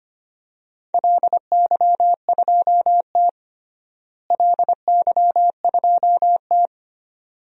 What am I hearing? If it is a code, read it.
Morse code, heard as a keyed tone, about 25 wpm, 700 Hz. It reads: LY2T LY2T
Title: DXCC-100 Call Signs - 4 Characters - Repeated 2x - 25wpm